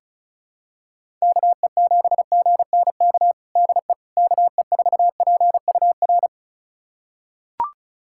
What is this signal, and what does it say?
Morse code, heard as a keyed tone, about 36 wpm, 700 Hz. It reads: KE7GNK DE KE4PUR
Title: USA Callsign DE Callsign - 6 Characters - 35wpm